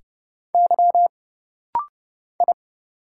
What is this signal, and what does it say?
Morse code, heard as a keyed tone, about 29 wpm, 700 Hz. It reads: Y I
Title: Single Letters - Rapid-Fire 30wpm